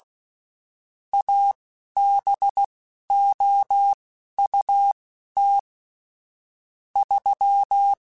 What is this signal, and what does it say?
Morse code, heard as a keyed tone, about 16 wpm, spaced as for 11 wpm, 800 Hz. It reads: ABOUT 3
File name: (spaced) / QSOs Spaced-mp3s / WD0BGP 1 @15 sp